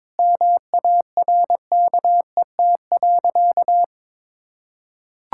Morse code, heard as a keyed tone, about 22 wpm, 700 Hz. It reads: MARKET.
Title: Sentences from Top 1600 Words 22wpm